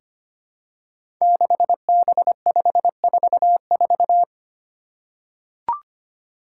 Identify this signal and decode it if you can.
Morse code, heard as a keyed tone, about 25 wpm, 700 Hz. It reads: 6B544
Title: Five Challenging Letter-Number Combinations 25wpm